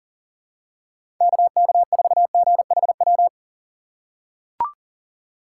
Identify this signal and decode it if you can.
Morse code, heard as a keyed tone, about 40 wpm, 700 Hz. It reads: KK4GHW
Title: USA Call Signs - 6 Characters - 4x Character Spacing - 40wpm